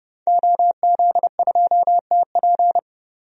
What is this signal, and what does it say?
Morse code, heard as a keyed tone, about 28 wpm, 690 Hz. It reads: OZ2TP
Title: DXCC-100 Call Signs - 5 Characters - 30wpm